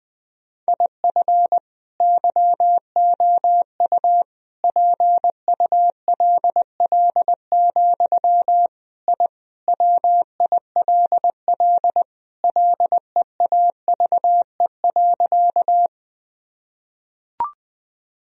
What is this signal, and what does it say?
Morse code, heard as a keyed tone, about 21 wpm, 700 Hz. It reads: IF YOU PULL, I WILL LEAVE.
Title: Sentences from Top 1000 Words - Review Part 2 - 20wpm